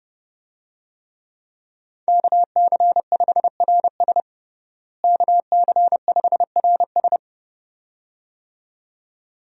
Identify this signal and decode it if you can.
Morse code, heard as a keyed tone, about 31 wpm, 700 Hz. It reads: KC5RS KC5RS
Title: USA Call Signs - 5 Characters - Repeated 2x - 30wpm